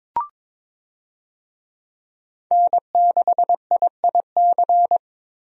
Morse code, heard as a keyed tone, about 22 wpm, 700 Hz. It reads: N6IIC
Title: USA Call Signs - 5 Characters - Repeated 3x - 22wpm